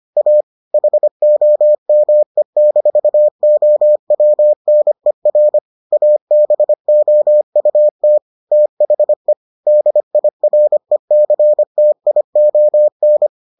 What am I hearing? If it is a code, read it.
Morse code, heard as a keyed tone, about 25 wpm, 600 Hz. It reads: A HOME-OWNER ABOUT THE DIRECTION